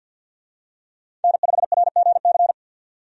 Morse code, heard as a keyed tone, about 51 wpm, 700 Hz. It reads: N5RGC
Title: USA Call Signs - 5 Characters - 50wpm